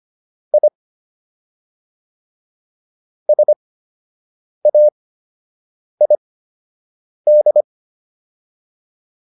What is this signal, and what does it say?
Morse code, heard as a keyed tone, about 25 wpm, spaced as for 7 wpm, 600 Hz. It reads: I SAID